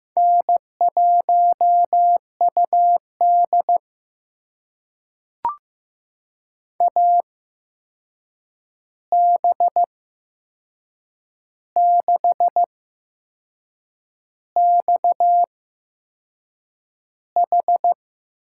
Morse code, heard as a keyed tone, about 15 wpm, spaced as for 6 wpm, 700 Hz. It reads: N1UD AB6XH
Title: USA Call Signs - 8x Character Spacing - 15wpm